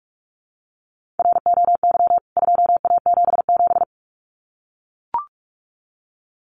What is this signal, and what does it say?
Morse code, heard as a keyed tone, about 44 wpm, 700 Hz. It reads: ROY 2A77
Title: CWT Exchange 45wpm